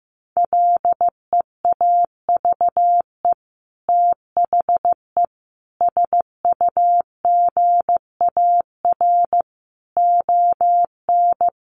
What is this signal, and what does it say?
Morse code, heard as a keyed tone, about 15 wpm, 700 Hz. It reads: LEAVE THE SUGAR ON
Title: Sentences from Top 1200 Words 15wpm